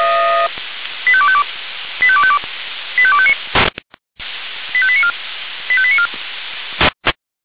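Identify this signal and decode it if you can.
Five-tone selective call (ZVEI), heard as quick runs of five tones, tones of about 70 ms. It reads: 85252 85252 85259 85894 85894
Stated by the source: ZVEI five-tone call 85894 received by SDR, channel 496.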